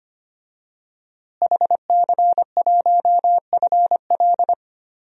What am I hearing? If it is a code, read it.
Morse code, heard as a keyed tone, about 25 wpm, 700 Hz. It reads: HC1FL
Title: DXCC-100 Call Signs - 5 Characters - Repeated 2x - 25wpm